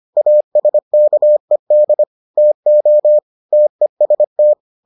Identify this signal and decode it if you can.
Morse code, heard as a keyed tone, about 25 wpm, 600 Hz. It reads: ASKED TO TEST